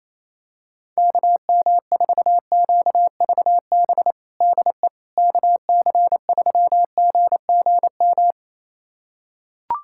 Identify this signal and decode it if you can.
Morse code, heard as a keyed tone, about 28 wpm, 700 Hz. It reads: KM4QVB DE KC3GGM
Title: USA Callsign DE Callsign - 6 Characters - 28wpm